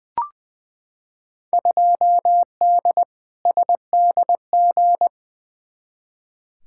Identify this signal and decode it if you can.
Morse code, heard as a keyed tone, about 20 wpm, 700 Hz. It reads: 2D SDG
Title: Field Day Exchanges Encore - 20wpm